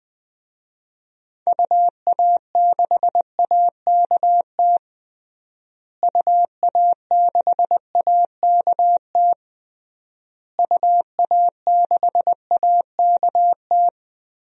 Morse code, heard as a keyed tone, about 19 wpm, 700 Hz. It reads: UA6AKT UA6AKT UA6AKT
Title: DXCC-100 Call Signs - 6 Characters - Repeated 3x - 20wpm